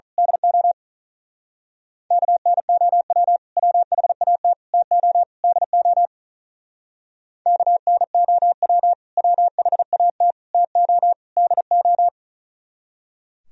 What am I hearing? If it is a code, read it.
Morse code, heard as a keyed tone, about 40 wpm, 700 Hz. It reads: DO KNOW WHAT TO DO KNOW WHAT TO DO
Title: Speed-Racing - Sets of 4 Words 35wpm